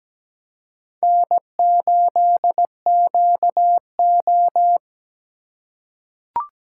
Morse code, heard as a keyed tone, about 17 wpm, 700 Hz. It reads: N8QO
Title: USA Call Signs - 4 Characters - 4x Character Spacing - 17wpm